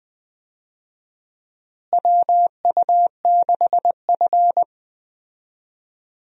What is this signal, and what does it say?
Morse code, heard as a keyed tone, about 20 wpm, 700 Hz. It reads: WU6F
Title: USA Call Signs - 4 Characters - Repeated 2x - 20wpm